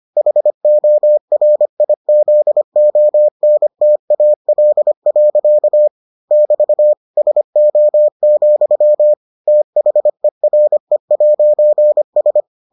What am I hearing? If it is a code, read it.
Morse code, heard as a keyed tone, about 25 wpm, 600 Hz. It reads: HORIZONTAL. <BT> SO, THERE'S